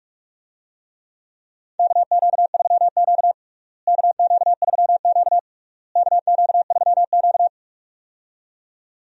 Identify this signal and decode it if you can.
Morse code, heard as a keyed tone, about 43 wpm, 700 Hz. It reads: KQ3Q KQ3Q KQ3Q
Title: USA Call Signs - 4 Characters - Repeated 3x - 45wpm